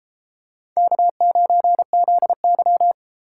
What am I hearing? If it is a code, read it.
Morse code, heard as a keyed tone, about 32 wpm, 700 Hz. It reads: K9ZY